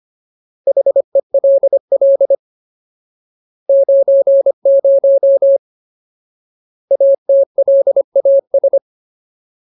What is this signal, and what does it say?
Morse code, heard as a keyed tone, about 25 wpm, 550 Hz. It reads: HELL 90 ATLAS
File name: Ultimate Word List / 25WPM Ultimate